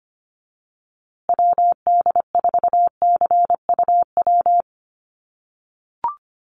Morse code, heard as a keyed tone, about 25 wpm, 700 Hz. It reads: WD4CUW
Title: USA Call Signs Repeated 3x 25wpm